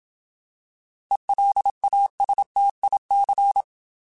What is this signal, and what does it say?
Morse code, heard as a keyed tone, about 27 wpm, 800 Hz. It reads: ELASTIC
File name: Alphabetical / Alphabetical mp3s / Es @25